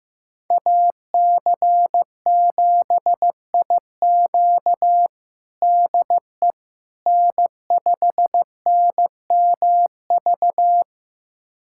Morse code, heard as a keyed tone, about 15 wpm, 700 Hz. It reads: AC7IQ DE N5NMV
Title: USA Callsign DE Callsign - 5 Characters - 15wpm